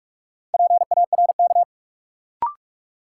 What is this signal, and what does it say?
Morse code, heard as a keyed tone, about 41 wpm, 700 Hz. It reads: PARK
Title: Sentences from Top 1300 Words - New Word Drill Alt - 45wpm